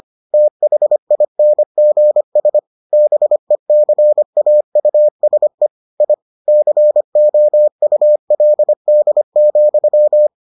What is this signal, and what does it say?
Morse code, heard as a keyed tone, about 25 wpm, 600 Hz. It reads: THINGS BECAUSE I COULD,